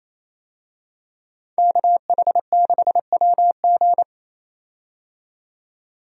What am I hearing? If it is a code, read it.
Morse code, heard as a keyed tone, about 28 wpm, 700 Hz. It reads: KH6WG
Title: DXCC-100 Call Signs - 5 Characters - 28wpm